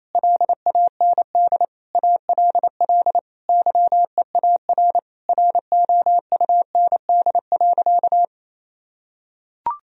Morse code, heard as a keyed tone, about 27 wpm, 700 Hz. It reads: LAND ALL YEAR ROUND.
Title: Sentences from Top 1000 Words - Review Part 1 - 28wpm